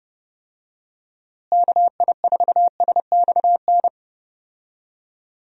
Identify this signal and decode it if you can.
Morse code, heard as a keyed tone, about 30 wpm, 700 Hz. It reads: KI4SXN
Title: USA Call Signs - 6 Characters - Repeated 3x - 30wpm